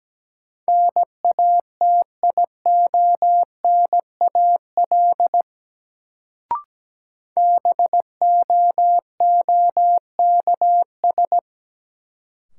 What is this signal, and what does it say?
Morse code, heard as a keyed tone, about 17 wpm, 700 Hz. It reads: NATIONAL BOOKS
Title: Sentences from Top 1200 Words - New Word Drill Alt - 17wpm